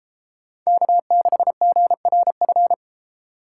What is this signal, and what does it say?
Morse code, heard as a keyed tone, about 31 wpm, 700 Hz. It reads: K6GRF